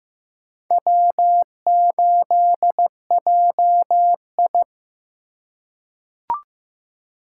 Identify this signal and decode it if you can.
Morse code, heard as a keyed tone, about 15 wpm, 700 Hz. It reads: W8JI
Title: USA Call Signs - 4 Characters - 4x Character Spacing - 15wpm